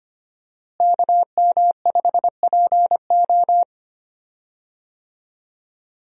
Morse code, heard as a keyed tone, about 25 wpm, 700 Hz. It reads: KM5PO
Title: USA Call Signs - 5 Characters - Repeated 2x - 25wpm